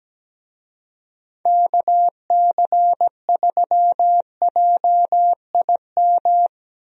Morse code, heard as a keyed tone, about 17 wpm, 700 Hz. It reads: KC3JIM
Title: USA Call Signs - 6 Characters - Repeated 3x - 17wpm